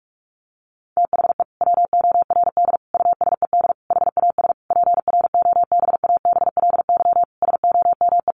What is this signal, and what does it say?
Morse code, heard as a keyed tone, about 46 wpm, 700 Hz. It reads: THE WORD USED HAS PROBABLY SOME
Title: Sentences from Top 1100 Words 45wpm